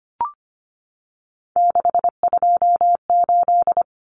Morse code, heard as a keyed tone, about 25 wpm, 700 Hz. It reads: 628
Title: Common Cut Numbers - 3-Digit 25wpm